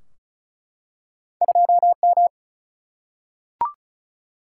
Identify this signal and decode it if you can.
Morse code, heard as a keyed tone, about 36 wpm, 700 Hz. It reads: 2M
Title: Single-Letter Single-Number 35wpm